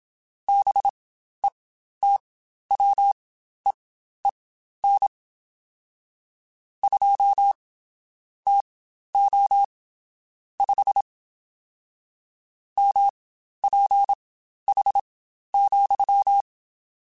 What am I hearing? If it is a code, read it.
Morse code, heard as a keyed tone, about 27 wpm, spaced as for 12 wpm, 800 Hz. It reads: BETWEEN 2 TO 5 MPH,
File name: Spaced Characters / Spaced Characters-mp3s / Weather 5 12 sp @25